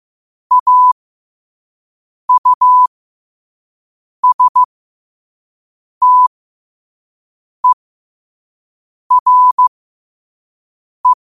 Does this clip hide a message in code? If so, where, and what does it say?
Yes: Morse code from 0.5 to 11.1 s, heard as a keyed tone, about 15 wpm, spaced as for 5 wpm, 1000 Hz. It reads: AUSTERE